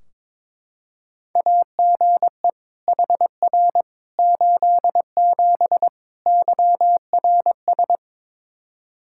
Morse code, heard as a keyed tone, about 21 wpm, 700 Hz. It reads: AGE HR 87 YRS
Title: Practice QSO - Exchange 3 - More Info - 22wpm